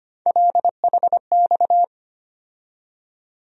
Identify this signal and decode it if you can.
Morse code, heard as a keyed tone, about 25 wpm, 700 Hz. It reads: LHX
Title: USA Call Sign Postfixes - 3 Letters 25wpm